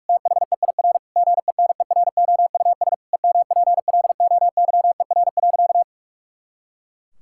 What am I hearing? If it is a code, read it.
Morse code, heard as a keyed tone, about 46 wpm, 700 Hz. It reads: THEIR GENEROUS EMPLOYER.
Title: Pangram Challenge v2 - 45wpm